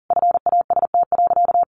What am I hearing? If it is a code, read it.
Morse code, heard as a keyed tone, about 40 wpm, 700 Hz. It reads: FAST.
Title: Sentences from Top 800 Words 40wpm